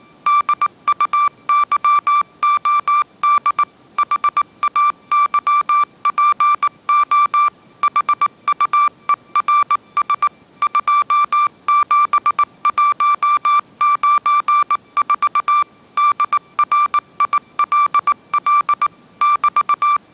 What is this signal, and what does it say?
Morse code, heard as a keyed tone, about 18 wpm, 1170 Hz. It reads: DUYOD HAYPO HUERS 27194 DRILL <BT>
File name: mrx083 20wpm